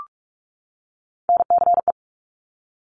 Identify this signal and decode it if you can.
Morse code, heard as a keyed tone, about 46 wpm, 690 Hz. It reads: NCE
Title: Top Three Letter Combinations in Words 45wpm